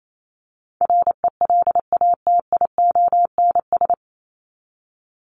Morse code, heard as a keyed tone, about 28 wpm, 700 Hz. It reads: RELATIONS
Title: ES:S Suffix - Top 100 Words with Base Words 28wpm